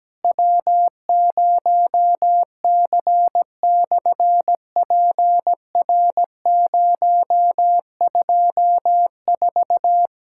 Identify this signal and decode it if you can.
Morse code, heard as a keyed tone, about 17 wpm, 700 Hz. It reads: W0C/PR024
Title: SOTA QSO Practice 17wpm